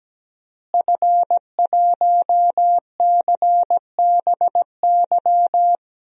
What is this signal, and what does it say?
Morse code, heard as a keyed tone, about 17 wpm, 700 Hz. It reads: F1CBY
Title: DXCC-100 Call Signs - 5 Characters - Repeated 3x - 17wpm